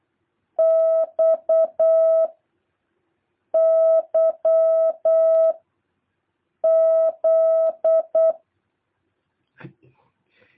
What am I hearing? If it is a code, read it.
Morse code, heard as a keyed tone, about 8 wpm, 650 Hz. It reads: X Y Z
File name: jh8cbh / 2023cw